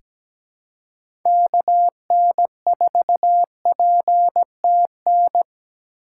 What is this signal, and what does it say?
Morse code, heard as a keyed tone, about 17 wpm, 700 Hz. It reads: KN4PTN
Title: USA Call Signs - 4x Character Spacing - 17wpm